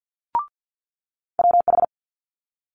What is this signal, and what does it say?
Morse code, heard as a keyed tone, about 50 wpm, 700 Hz. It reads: WH
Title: Multi-letter Phonemes Quick Fire 50wpm